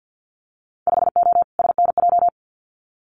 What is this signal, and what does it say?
Morse code, heard as a keyed tone, about 50 wpm, 700 Hz. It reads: 5O SNJ
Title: Winter Field Day Exchanges Encore 50wpm